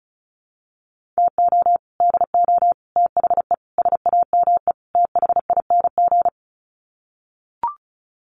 Morse code, heard as a keyed tone, about 34 wpm, 700 Hz. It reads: TO DO THE SAME THING
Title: Sets of 5 Words Encore 35wpm